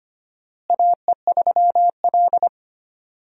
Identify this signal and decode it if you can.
Morse code, heard as a keyed tone, about 25 wpm, 700 Hz. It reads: AE3L
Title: USA Call Signs - 4 Characters - Repeated 3x - 25wpm